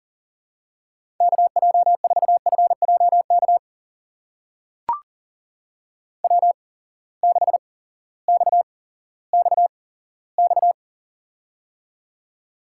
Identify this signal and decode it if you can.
Morse code, heard as a keyed tone, about 38 wpm, spaced as for 18 wpm, 700 Hz. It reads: KJ4FJK W6XXX
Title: USA Call Signs - 8x Character Spacing - 40wpm